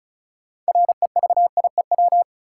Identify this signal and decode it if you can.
Morse code, heard as a keyed tone, about 34 wpm, 700 Hz. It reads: REVIEW
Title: Sentences from Top 900 Words - New Word Drill - 35wpm